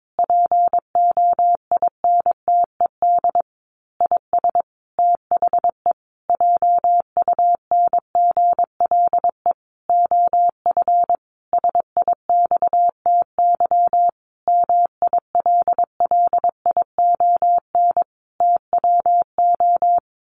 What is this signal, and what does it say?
Morse code, heard as a keyed tone, about 22 wpm, 700 Hz. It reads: POINTED IS THE JUNGLE OF SIXTY MILLION TWO
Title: A Sound Of Thunder 22wpm